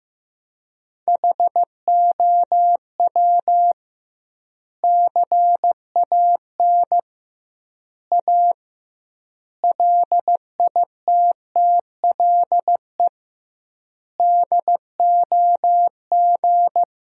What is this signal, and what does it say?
Morse code, heard as a keyed tone, about 15 wpm, 700 Hz. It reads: HOW CAN A LITTLE DOG
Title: 2x Word-Spacing - Sentences from Top 500 Words - 15wpm